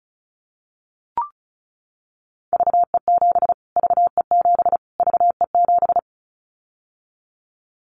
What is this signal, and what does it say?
Morse code, heard as a keyed tone, about 34 wpm, 700 Hz. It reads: VE7 VE7 VE7
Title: Callsign Prefixes for Top 25 DXCC - Repeated 3x - 35wpm